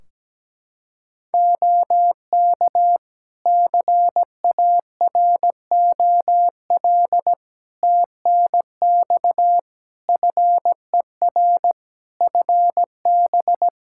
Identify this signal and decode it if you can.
Morse code, heard as a keyed tone, about 17 wpm, 700 Hz. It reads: OK CAROL TNX FER FB
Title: K1USN SST QSO Practice 17wpm